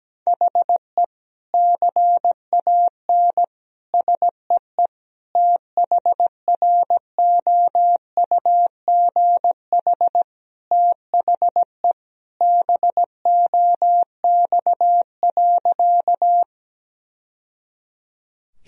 Morse code, heard as a keyed tone, about 17 wpm, 700 Hz. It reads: HE CAN SEE THROUGH THE BOX.
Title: Sentences from Top 1000 Words - Review Part 2 - 17wpm